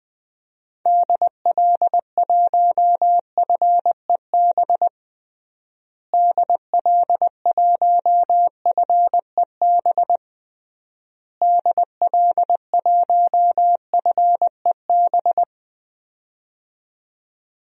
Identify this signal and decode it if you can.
Morse code, heard as a keyed tone, about 21 wpm, 700 Hz. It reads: DL1FEB DL1FEB DL1FEB
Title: DXCC-100 Call Signs - 6 Characters - Repeated 3x - 20wpm